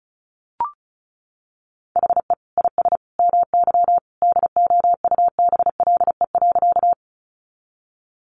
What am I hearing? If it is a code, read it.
Morse code, heard as a keyed tone, about 34 wpm, 700 Hz. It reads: HE IS MY DOUBLE.
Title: Sentences from Top 1400 Words 35wpm